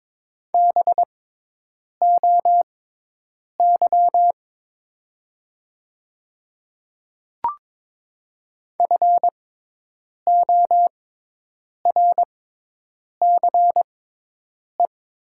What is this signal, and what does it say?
Morse code, heard as a keyed tone, about 22 wpm, spaced as for 8 wpm, 700 Hz. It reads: BOY FORCE
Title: 6x Character Spacing - Top 500 Words - 22wpm